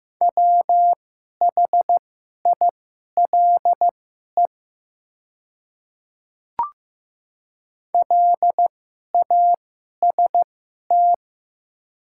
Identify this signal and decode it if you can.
Morse code, heard as a keyed tone, about 15 wpm, spaced as for 11 wpm, 700 Hz. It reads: WHILE LAST
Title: 2x Character Spacing - Top 500 Words - 15wpm